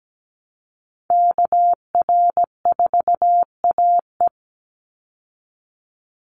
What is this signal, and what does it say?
Morse code, heard as a keyed tone, about 17 wpm, 700 Hz. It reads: KR4AE
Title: Call Signs Repeated 2x - 17wpm